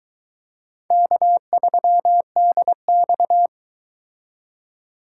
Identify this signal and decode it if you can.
Morse code, heard as a keyed tone, about 24 wpm, 700 Hz. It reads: K3DX